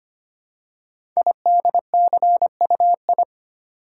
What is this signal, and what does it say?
Morse code, heard as a keyed tone, about 25 wpm, 690 Hz. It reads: IDCUI
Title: Code Groups - 5 Letter:Numbers - 25wpm